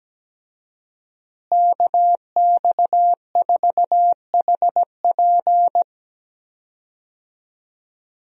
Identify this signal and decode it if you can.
Morse code, heard as a keyed tone, about 17 wpm, 700 Hz. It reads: KX4HP
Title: USA Call Signs - 5 Characters - Repeated 3x - 17wpm